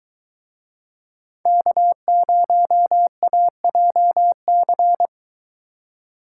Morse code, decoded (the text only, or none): K0AJC